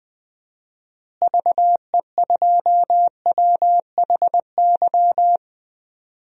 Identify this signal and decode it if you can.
Morse code, heard as a keyed tone, about 21 wpm, 700 Hz. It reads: VE2WHY
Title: DXCC-100 Call Signs - 6 Characters - Repeated 3x - 20wpm